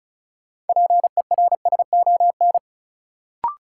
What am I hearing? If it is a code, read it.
Morse code, heard as a keyed tone, about 36 wpm, 700 Hz. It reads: PERSON
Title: Sentences from Top 400 Words - New Word Drill - 35wpm